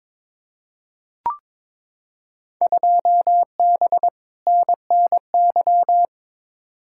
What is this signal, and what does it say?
Morse code, heard as a keyed tone, about 22 wpm, 700 Hz. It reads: 2B NNY
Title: Field Day Exchanges Encore - 22wpm